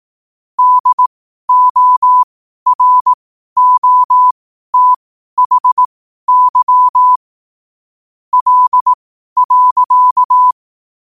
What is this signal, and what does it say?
Morse code, heard as a keyed tone, about 18 wpm, spaced as for 12 wpm, 1000 Hz. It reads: DOROTHY L.
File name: qotdfiles / 12wpm